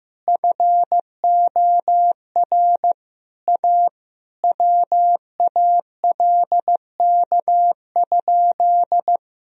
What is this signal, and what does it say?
Morse code, heard as a keyed tone, about 15 wpm, 700 Hz. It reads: FOR A WALK?
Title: Sentences from Top 1000 Words - Review Part 1 - 15wpm